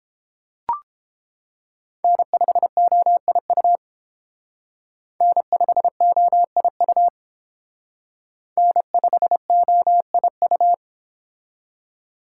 Morse code, heard as a keyed tone, about 28 wpm, 700 Hz. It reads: N5OIU N5OIU N5OIU